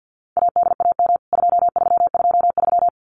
Morse code, heard as a keyed tone, about 50 wpm, 700 Hz. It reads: ADAM 2323
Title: CWT Exchange 50wpm